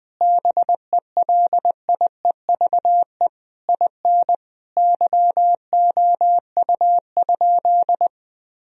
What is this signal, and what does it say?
Morse code, heard as a keyed tone, about 21 wpm, 700 Hz. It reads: BELIEVE IN YOU?
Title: Sentences from Top 1000 Words - Review Part 3 - 20wpm